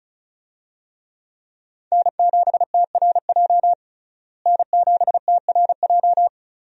Morse code, heard as a keyed tone, about 36 wpm, 700 Hz. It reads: N7TRJ N7TRJ
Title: USA Call Signs - 5 Characters - Repeated 3x - 35wpm